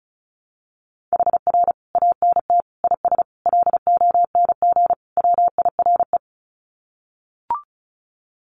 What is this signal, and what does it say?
Morse code, heard as a keyed tone, about 34 wpm, 700 Hz. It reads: HR ANT IS LONG WIRE
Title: Common QSO Phrases 35wpm